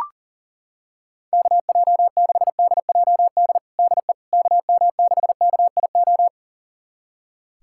Morse code, heard as a keyed tone, about 40 wpm, 700 Hz. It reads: KJ6DJD DE KM6KIO
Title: USA Callsign DE Callsign 40wpm